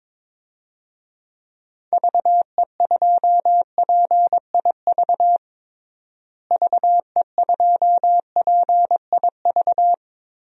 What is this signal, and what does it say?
Morse code, heard as a keyed tone, about 22 wpm, 700 Hz. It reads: VE2PIV VE2PIV
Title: DXCC-100 Call Signs - 6 Characters - Repeated 3x - 22wpm